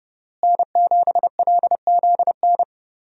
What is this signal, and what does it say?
Morse code, heard as a keyed tone, about 30 wpm, 700 Hz. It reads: N7LZN
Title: USA Call Signs - 5 Characters - Repeated 2x - 30wpm